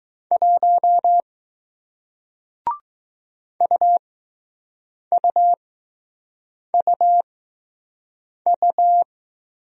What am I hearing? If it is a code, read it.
Morse code, heard as a keyed tone, about 20 wpm, spaced as for 6 wpm, 700 Hz. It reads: 1 UUUU